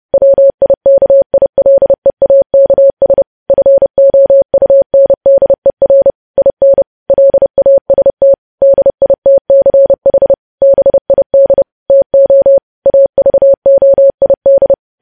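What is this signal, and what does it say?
Morse code, heard as a keyed tone, about 30 wpm, 550 Hz. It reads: WIKILEAKS FOUNDER IN LAST DITCH BID TO AVOID
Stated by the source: Morse news headlines sent at 30.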